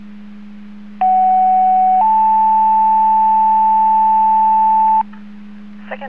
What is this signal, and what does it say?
Two-tone page, heard as a one-second tone, then a three-second tone, about 750 then 880 Hz. It is heard from 1.0 to 5.0 s.
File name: Portage 154.385 Paging